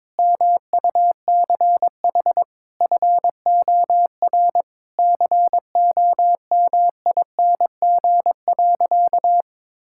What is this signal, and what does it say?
Morse code, heard as a keyed tone, about 22 wpm, 700 Hz. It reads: MUCH FOR COMING.
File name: practice / 22wpm / Sentences with 4-Word Phrases - 22wpm